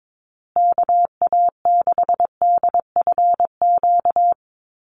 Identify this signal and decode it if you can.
Morse code, heard as a keyed tone, about 22 wpm, 700 Hz. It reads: KA6DFQ
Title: USA Call Signs Repeated 2x 22wpm